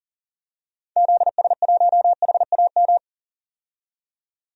Morse code, heard as a keyed tone, about 38 wpm, 700 Hz. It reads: ZS1HAM
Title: DXCC-100 Call Signs - 6 Characters - 40wpm